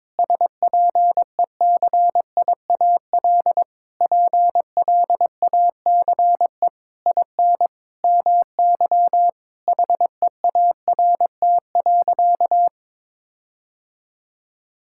Morse code, heard as a keyed tone, about 22 wpm, 700 Hz. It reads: SPECIAL PLACE IN MY HEART.
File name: Sentences with 9-Word Phrases - 22wpm